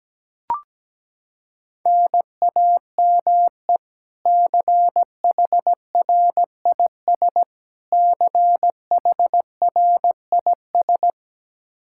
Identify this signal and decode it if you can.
Morse code, heard as a keyed tone, about 17 wpm, 700 Hz. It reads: NAME CHRIS CHRIS
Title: Practice QSO - Exchange 1 - Beginning - 17wpm